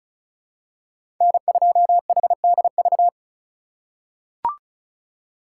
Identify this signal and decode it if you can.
Morse code, heard as a keyed tone, about 36 wpm, 700 Hz. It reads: N2HDV
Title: USA Call Signs - 8x Character Spacing - 35wpm